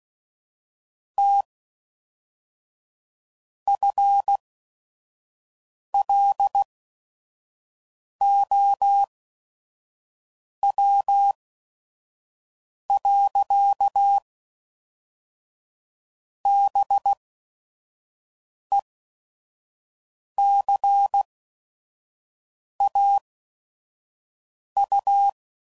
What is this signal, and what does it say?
Morse code, heard as a keyed tone, about 16 wpm, 800 Hz. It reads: T F L O W . B E C A U